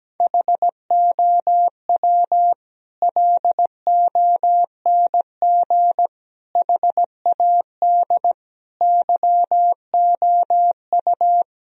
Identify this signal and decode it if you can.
Morse code, heard as a keyed tone, about 17 wpm, 700 Hz. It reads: HOW LONG HAD YOU
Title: Sentences from Top 1000 Words - Review Part 3 - 17wpm